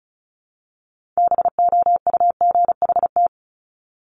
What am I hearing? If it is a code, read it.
Morse code, heard as a keyed tone, about 36 wpm, 700 Hz. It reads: BOUGHT
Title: Multi-letter Phonemes Repeated 3x 35wpm